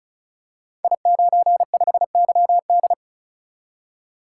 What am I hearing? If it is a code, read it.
Morse code, heard as a keyed tone, about 36 wpm, 700 Hz. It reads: I95YD
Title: Code Groups - 5 Letter:Numbers - 35wpm